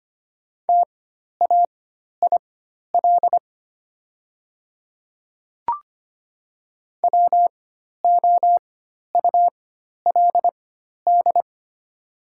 Morse code, heard as a keyed tone, about 25 wpm, spaced as for 12 wpm, 700 Hz. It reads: TAIL WOULD
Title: 4x Character Spacing - Top 500 Words - 25wpm